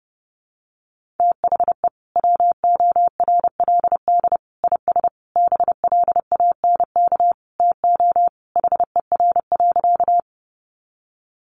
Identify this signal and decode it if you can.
Morse code, heard as a keyed tone, about 30 wpm, 700 Hz. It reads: THE WORLD IS BLANK TO HER.
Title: Sentences from Top 1500 Words 30wpm